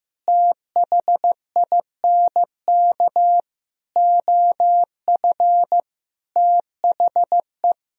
Morse code, heard as a keyed tone, about 15 wpm, 700 Hz. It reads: THINK OF THE
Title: Sentences from Top 1000 Words - Review Part 4 - 15wpm